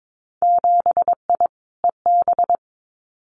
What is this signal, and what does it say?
Morse code, heard as a keyed tone, about 22 wpm, 700 Hz. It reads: 7I EB
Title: Winter Field Day Exchanges Encore 22wpm